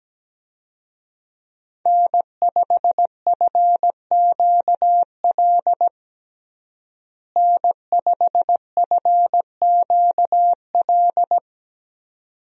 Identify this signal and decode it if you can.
Morse code, heard as a keyed tone, about 17 wpm, 700 Hz. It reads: N5FQL N5FQL
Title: USA Call Signs - 5 Characters - Repeated 2x - 17wpm